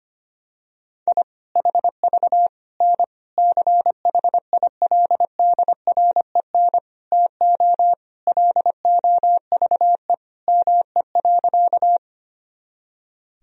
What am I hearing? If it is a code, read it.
Morse code, heard as a keyed tone, about 25 wpm, 700 Hz. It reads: I HV N CHILDREN TO LOVE ME.
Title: Sentences from Top 400 Words and Abbreviations 25wpm